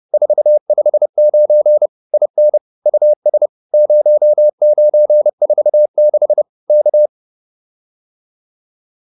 Morse code, heard as a keyed tone, about 30 wpm, 600 Hz. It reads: 459 IN US 0946 K
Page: advanced-practice-files